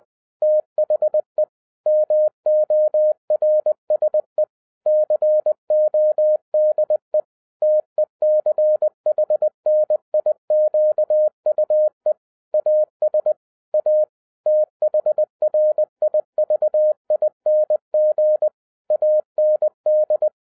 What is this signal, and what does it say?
Morse code, heard as a keyed tone, about 20 wpm, 610 Hz. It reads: THE MORSE CODE TECHNIQUE AS A THRIVING AND